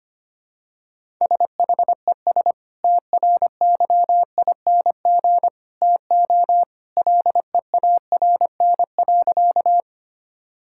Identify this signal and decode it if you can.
Morse code, heard as a keyed tone, about 25 wpm, 700 Hz. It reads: SHES TRYING TO LEARN.
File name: practice / 25wpm / Sentences with 2-Word Phrases - 25wpm